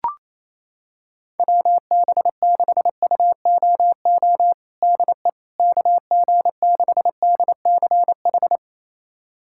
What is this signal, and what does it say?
Morse code, heard as a keyed tone, about 28 wpm, 700 Hz. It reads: WB6UOO DE KG6DCH
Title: USA Callsign DE Callsign 28wpm